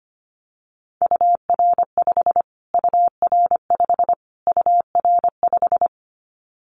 Morse code, heard as a keyed tone, about 25 wpm, 700 Hz. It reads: UR5 UR5 UR5
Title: Callsign Prefixes for Top 25 DXCC - Repeated 3x - 25wpm